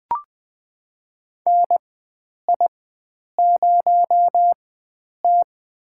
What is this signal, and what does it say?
Morse code, heard as a keyed tone, about 20 wpm, spaced as for 9 wpm, 700 Hz. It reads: NI0T
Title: USA Call Signs - 4x Character Spacing - 20wpm